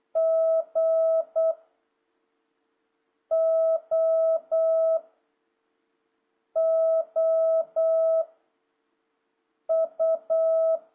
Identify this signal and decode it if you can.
Morse code, heard as a keyed tone, about 8 wpm, 650 Hz. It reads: G O O U
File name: jh8cbh / 2026cw